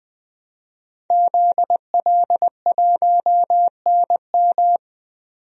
Morse code, heard as a keyed tone, about 21 wpm, 700 Hz. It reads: ZL1NM
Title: DXCC-100 Call Signs - 5 Characters - 20wpm